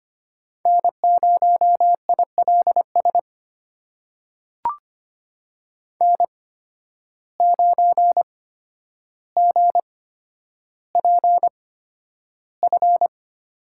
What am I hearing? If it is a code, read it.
Morse code, heard as a keyed tone, about 25 wpm, spaced as for 11 wpm, 700 Hz. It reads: N0ILS N9GPF
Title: USA Call Signs - 5 Characters - 8x Character Spacing - 25wpm